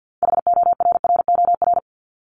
Morse code, heard as a keyed tone, about 50 wpm, 700 Hz. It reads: HORROR
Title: Top 1501-1750 Words 50wpm